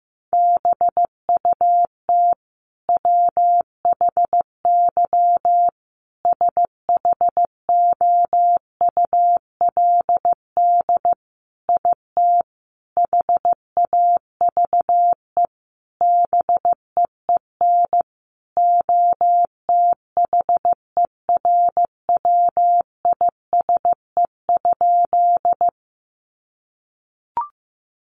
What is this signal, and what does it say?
Morse code, heard as a keyed tone, about 15 wpm, 700 Hz. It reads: BUT WHY SHOULD IT HAVE BEEN OTHERWISE?
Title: Sentences from Top 1200 Words 15wpm